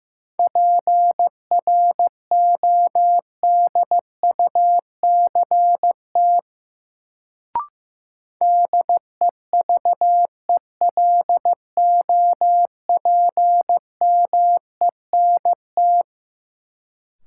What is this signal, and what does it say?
Morse code, heard as a keyed tone, about 15 wpm, 700 Hz. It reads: PRODUCT DEVELOPMENT
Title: Sentences from Top 1400 Words - New Word Drill Alt - 15wpm